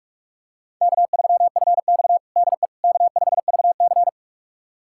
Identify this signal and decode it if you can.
Morse code, heard as a keyed tone, about 42 wpm, 700 Hz. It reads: K3FX DE K5VC
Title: USA Callsign DE Callsign - 4 Characters - 45wpm